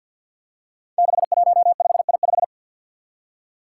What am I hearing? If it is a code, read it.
Morse code, heard as a keyed tone, about 47 wpm, 700 Hz. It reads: 615I5
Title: Five Challenging Letter-Number Combinations 50wpm